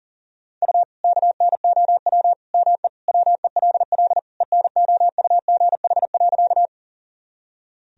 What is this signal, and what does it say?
Morse code, heard as a keyed tone, about 37 wpm, 700 Hz. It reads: U KNOW ME WELL ENOUGH.
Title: Sentences from Top 400 Words and Abbreviations 40wpm